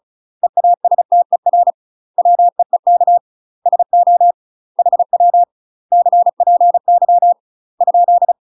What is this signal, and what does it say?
Morse code, heard as a keyed tone, about 34 wpm, 700 Hz. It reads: EASTER WEEK SO HW CPY ?
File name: UK MASTER QSOs TRAINING / 35WPM / UK MASTER QSOs 0021